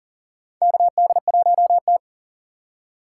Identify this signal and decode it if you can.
Morse code, heard as a keyed tone, about 40 wpm, 700 Hz. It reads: KD1T
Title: USA Call Signs - 4 Characters - 4x Character Spacing - 40wpm